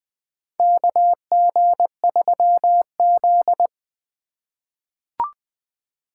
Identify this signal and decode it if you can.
Morse code, heard as a keyed tone, about 20 wpm, 700 Hz. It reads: KG3Z
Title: USA Call Signs - 2x Character Spacing - 20wpm